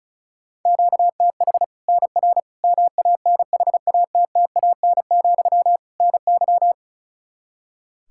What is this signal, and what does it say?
Morse code, heard as a keyed tone, about 34 wpm, 700 Hz. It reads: QTH NR MANHATTAN, NY
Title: Practice QSO - Exchange 1 - Beginning - 35wpm